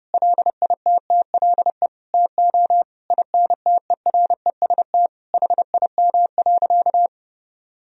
Morse code, heard as a keyed tone, about 30 wpm, 700 Hz. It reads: LITTLE TO INTEREST HIM.
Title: Sentences from Top 1000 Words - Review Part 2 - 30wpm